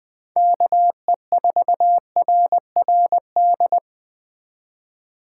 Morse code, heard as a keyed tone, about 23 wpm, 700 Hz. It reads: KE4RRD
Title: Speed Racing - USA 2x3 Call Signs 20wpm